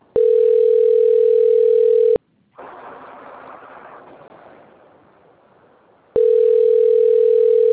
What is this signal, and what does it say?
Ringback tone (North America), 440 + 480 Hz.